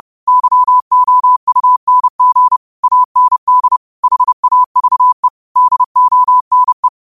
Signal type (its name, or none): Morse code